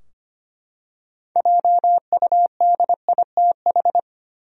Morse code, heard as a keyed tone, about 25 wpm, 700 Hz. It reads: JUDITH
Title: USA Common Names - 25wpm